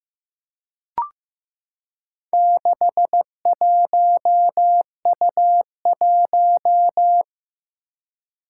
Morse code, heard as a keyed tone, about 15 wpm, 700 Hz. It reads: 61U1
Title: Four Challenging Letter-Number Combinations 15wpm